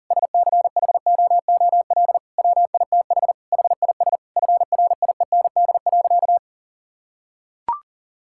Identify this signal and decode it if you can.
Morse code, heard as a keyed tone, about 40 wpm, 700 Hz. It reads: SCHOOL WITH HIS FRIEND.
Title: Sentences from Top 1600 Words 40wpm